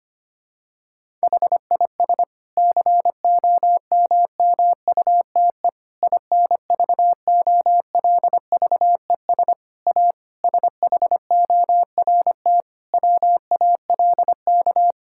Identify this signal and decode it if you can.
Morse code, heard as a keyed tone, about 25 wpm, 700 Hz. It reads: HIS COMMUTE INVOLVES A SHORT WALK
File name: practice / 25wpm / Difficult Sentences with Hints - 25wpm